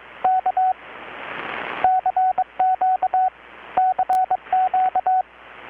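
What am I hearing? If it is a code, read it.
Morse code, heard as a keyed tone, about 22 wpm, 750 Hz. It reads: K CQ CQ
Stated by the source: Morse from KPH's Night of Nights XX maritime radiotelegraphy event, received in Chiapas, Mexico about 3,800 km away on a Kenwood R-600 with a grounded 10 m long-wire antenna.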